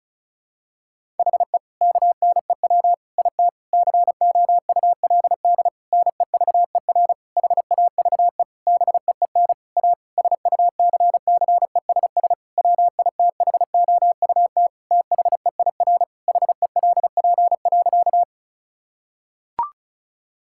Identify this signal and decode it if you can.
Morse code, heard as a keyed tone, about 36 wpm, 700 Hz. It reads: HE KNEW IT COULD NEVER HAVE BEEN A SUCCESS WITHOUT THEIR HELP.
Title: Sentences from Top 1000 Words - Review Part 6 - 35wpm